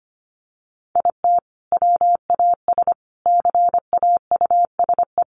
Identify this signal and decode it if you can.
Morse code, heard as a keyed tone, about 25 wpm, 700 Hz. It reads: IT WAS CAUSE
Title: Sentences from Top 1000 Words 25wpm